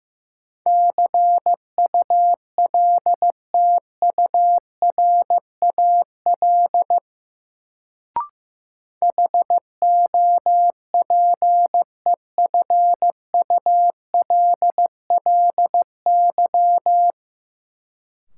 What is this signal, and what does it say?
Morse code, heard as a keyed tone, about 15 wpm, 700 Hz. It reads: CULTURAL HOPEFULLY
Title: Sentences from Top 1500 Words - New Word Drill - 15wpm